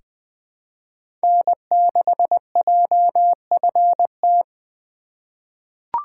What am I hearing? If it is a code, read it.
Morse code, heard as a keyed tone, about 20 wpm, 700 Hz. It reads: N6JFT
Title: USA Call Signs - 5 Characters - 2x Character Spacing - 20wpm